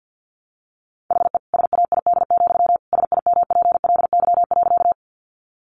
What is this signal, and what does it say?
Morse code, heard as a keyed tone, about 50 wpm, 700 Hz. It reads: HE SAID, SIMPLY.
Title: Sentences from Top 600 Words 50wpm